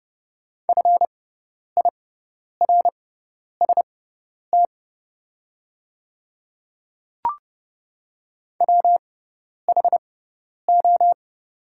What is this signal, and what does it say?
Morse code, heard as a keyed tone, about 30 wpm, spaced as for 10 wpm, 700 Hz. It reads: FIRST WHO
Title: 6x Character Spacing - Top 100 Words - 30wpm